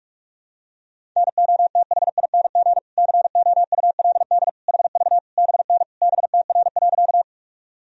Morse code, heard as a keyed tone, about 46 wpm, 690 Hz. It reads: NOTHING COULD HV BN BTR.
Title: Sentences from Top 500 Words and Abbreviations 45wpm